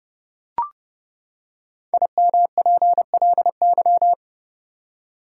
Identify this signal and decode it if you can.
Morse code, heard as a keyed tone, about 30 wpm, 700 Hz. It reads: IMPLY
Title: Five Letter Words - 30wpm